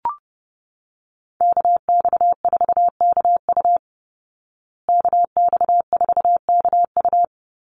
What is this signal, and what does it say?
Morse code, heard as a keyed tone, about 30 wpm, 700 Hz. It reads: KX4KU KX4KU
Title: Call Signs Repeated 2x - 30wpm